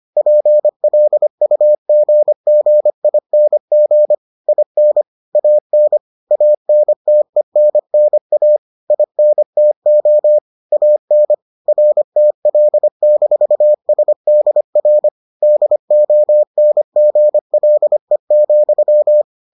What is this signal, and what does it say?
Morse code, heard as a keyed tone, about 25 wpm, 600 Hz. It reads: PLUGGING IN AN ANTENNA INTO AN RTL-SDR DONGLE,